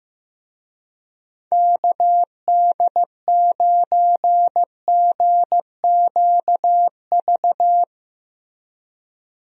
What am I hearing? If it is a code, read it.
Morse code, heard as a keyed tone, about 15 wpm, 700 Hz. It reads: KD9GQV
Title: USA Call Signs - 6 Characters - Repeated 3x - 15wpm